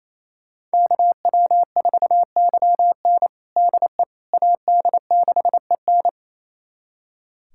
Morse code, heard as a keyed tone, about 28 wpm, 700 Hz. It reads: KW4YN DE AD6EN
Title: USA Callsign DE Callsign - 5 Characters - 28wpm